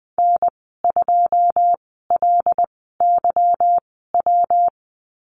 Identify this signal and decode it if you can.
Morse code, heard as a keyed tone, about 20 wpm, spaced as for 15 wpm, 700 Hz. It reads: N2LYW
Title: 179 Lesson 29g - Slash - Call Signs 20wpm